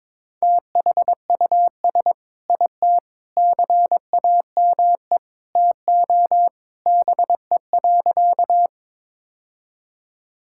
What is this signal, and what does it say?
Morse code, heard as a keyed tone, about 22 wpm, 700 Hz. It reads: THUS IT CAME TO BE.
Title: Sentences from Top 1000 Words - Review Part 5 - 22wpm